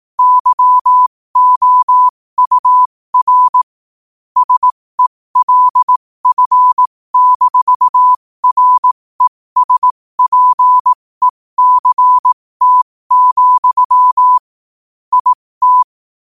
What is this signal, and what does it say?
Morse code, heard as a keyed tone, about 18 wpm, spaced as for 15 wpm, 1000 Hz. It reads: YOUR SELF-RESPECT, IT
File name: qotdfiles / 15wpm